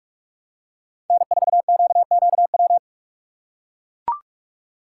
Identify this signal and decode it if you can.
Morse code, heard as a keyed tone, about 41 wpm, 700 Hz. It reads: N4QQW
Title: USA Call Signs - 5 Characters - 3x Character Spacing - 45wpm